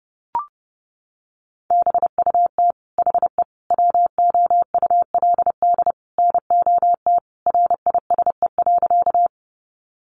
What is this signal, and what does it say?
Morse code, heard as a keyed tone, about 28 wpm, 700 Hz. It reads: BUT HE WOULD NOT RISE.
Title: Sentences from Top 900 Words 30wpm